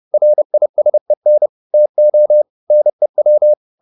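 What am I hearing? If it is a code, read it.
Morse code, heard as a keyed tone, about 30 wpm, 600 Hz. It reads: RISEN TO NEW
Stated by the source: Morse code at 30 wpm, the novel Never by Ken Follett in English, from CW med Gnister.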